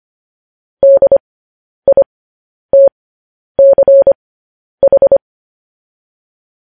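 Morse code, heard as a keyed tone, about 25 wpm, spaced as for 10 wpm, 550 Hz. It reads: DITCH
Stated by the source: Morse code news headlines broadcast.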